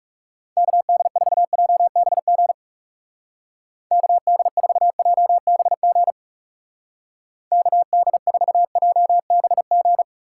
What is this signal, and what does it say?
Morse code, heard as a keyed tone, about 36 wpm, 700 Hz. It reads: KD4JBG KD4JBG KD4JBG
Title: Speed Racing - USA 2x3 Call Signs 30wpm